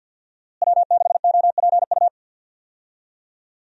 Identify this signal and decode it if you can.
Morse code, heard as a keyed tone, about 49 wpm, 700 Hz. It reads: W6OPU
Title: USA Call Signs - 5 Characters - Repeated 3x - 50wpm